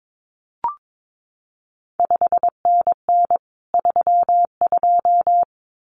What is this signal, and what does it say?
Morse code, heard as a keyed tone, about 22 wpm, 700 Hz. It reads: 5NN 32
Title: CQ WW DX Contest Exchange Encore - 22wpm